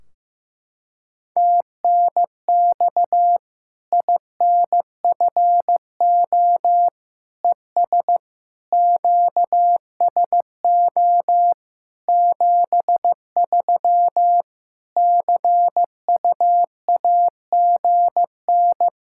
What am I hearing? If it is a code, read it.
Morse code, heard as a keyed tone, about 15 wpm, 700 Hz. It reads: TNX INFO ES QSO 73 CUAGN
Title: Practice QSO - Exchange 4 - Ending - 15wpm